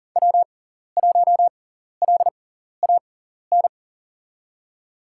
Morse code, heard as a keyed tone, about 37 wpm, spaced as for 14 wpm, 700 Hz. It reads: W1LAN
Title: USA Call Signs - 5 Characters - 6x Character Spacing - 40wpm